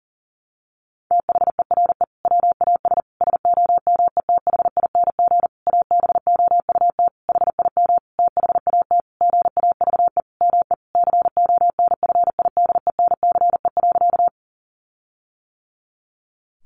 Morse code, heard as a keyed tone, about 40 wpm, 700 Hz. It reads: THERE WAS SOMETHING ABOUT HIM THAT GAVE ME CONFIDENCE.
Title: Sentences from Top 1500 Words 40wpm